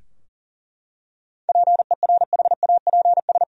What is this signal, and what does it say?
Morse code, heard as a keyed tone, about 40 wpm, 700 Hz. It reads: PERHAPS
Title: Sentences from Top 1000 Words - Review Part 2 - 40wpm